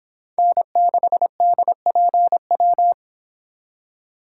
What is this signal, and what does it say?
Morse code, heard as a keyed tone, about 25 wpm, 700 Hz. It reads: N6DPW